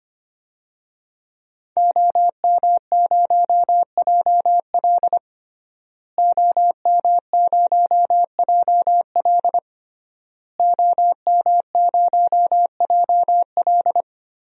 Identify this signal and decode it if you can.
Morse code, heard as a keyed tone, about 25 wpm, 700 Hz. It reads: OM0JL OM0JL OM0JL
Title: DXCC-100 Call Signs - 5 Characters - Repeated 3x - 25wpm